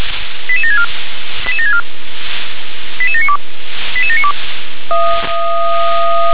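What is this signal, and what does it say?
Five-tone selective call (ZVEI), heard as quick runs of five tones, tones of about 70 ms. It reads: 88764 88764 88792 88792
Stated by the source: ZVEI five-tone selective call received by SDR.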